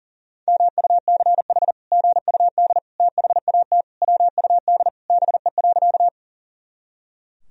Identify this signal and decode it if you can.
Morse code, heard as a keyed tone, about 40 wpm, 700 Hz. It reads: MUCH GUD THAT WUD BE.
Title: Sentences from Top 200 Words and Abbreviations 40wpm